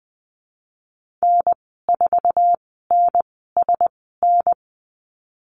Morse code, heard as a keyed tone, about 20 wpm, spaced as for 14 wpm, 700 Hz. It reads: N4NSN
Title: 055 Lesson 10g - Letter r - Call Signs 20wpm